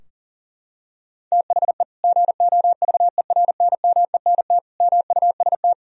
Morse code, heard as a keyed tone, about 40 wpm, 700 Hz. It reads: THE GOVERNMENT MUST
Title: Sentences from Top 1000 Words - Review Part 3 - 40wpm